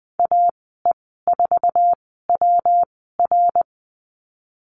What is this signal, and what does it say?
Morse code, heard as a keyed tone, about 20 wpm, spaced as for 15 wpm, 700 Hz. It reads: AE4WR
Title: 128 Lesson 21g - Number 6 - Call Signs 20wpm